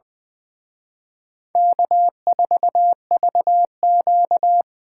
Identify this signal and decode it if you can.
Morse code, heard as a keyed tone, about 24 wpm, 700 Hz. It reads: K4VQ
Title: Speed Racing - USA 1x2 Call Signs 20wpm